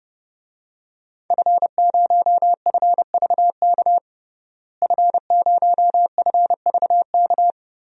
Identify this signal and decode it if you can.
Morse code, heard as a keyed tone, about 30 wpm, 700 Hz. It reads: F0FVK F0FVK
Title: DXCC-100 Call Signs - 5 Characters - Repeated 3x - 30wpm